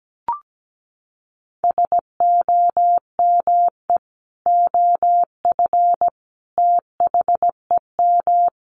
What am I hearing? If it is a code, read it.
Morse code, heard as a keyed tone, about 17 wpm, 700 Hz. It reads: SOME OF THEM
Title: Sets of 3 Words 17wpm